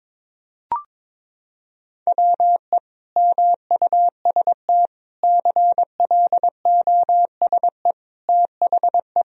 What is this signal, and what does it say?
Morse code, heard as a keyed tone, about 22 wpm, 700 Hz. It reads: WE MUST CLOSE THE
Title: Sentences from Top 1000 Words - Review Part 1 - 22wpm